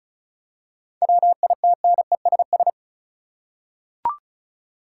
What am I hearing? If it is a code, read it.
Morse code, heard as a keyed tone, about 36 wpm, 700 Hz. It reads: WITNESS
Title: Seven Letter Words - 35wpm